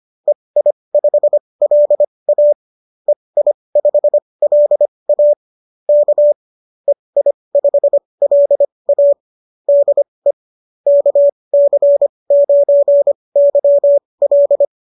Morse code, heard as a keyed tone, about 25 wpm, spaced as for 20 wpm, 590 Hz. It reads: EI5LA EI5LA K EI5LA DE KC9YL